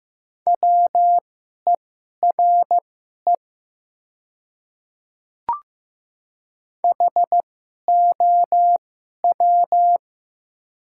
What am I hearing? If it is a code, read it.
Morse code, heard as a keyed tone, about 15 wpm, spaced as for 11 wpm, 700 Hz. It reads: WERE HOW
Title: 2x Character Spacing - Top 100 Words - 15wpm